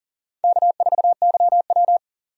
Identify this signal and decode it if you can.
Morse code, heard as a keyed tone, about 40 wpm, 700 Hz. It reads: K4YW
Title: USA Call Signs - 4 Characters - 40wpm